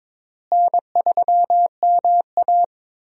Morse code, heard as a keyed tone, about 22 wpm, 700 Hz. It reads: N3MA
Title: USA Call Signs - 4 Characters - Repeated 3x - 22wpm